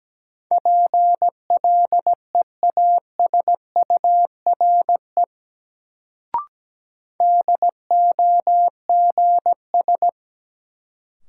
Morse code, heard as a keyed tone, about 17 wpm, 700 Hz. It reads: PLEASURE DOGS
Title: Sentences from Top 1600 Words - New Word Drill Alt - 17wpm